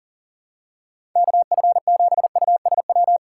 Morse code, heard as a keyed tone, about 40 wpm, 690 Hz. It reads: KF7USW
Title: USA Call Signs - 6 Characters - 2x Character Spacing - 40wpm